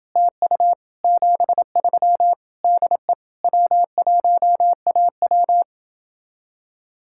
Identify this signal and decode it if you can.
Morse code, heard as a keyed tone, about 30 wpm, 700 Hz. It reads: TU 73 DE W1AW